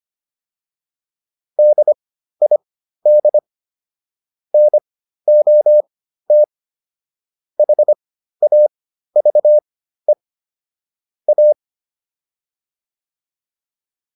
Morse code, heard as a keyed tone, about 25 wpm, spaced as for 13 wpm, 600 Hz. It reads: DID NOT HAVE A